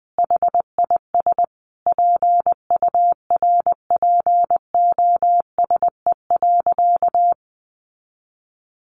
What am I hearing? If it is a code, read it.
Morse code, heard as a keyed tone, about 20 wpm, 700 Hz. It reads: HIS PURPOSE.
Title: Sentences from Top 1400 Words 20wpm